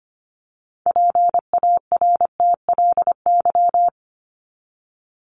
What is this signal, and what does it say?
Morse code, heard as a keyed tone, about 25 wpm, 700 Hz. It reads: PARTLY
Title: LY Suffix - Top 100 Words with Base Words 25wpm